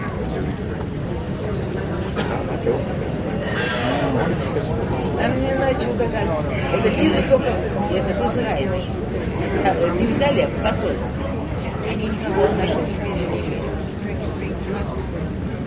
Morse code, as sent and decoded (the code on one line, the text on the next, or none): none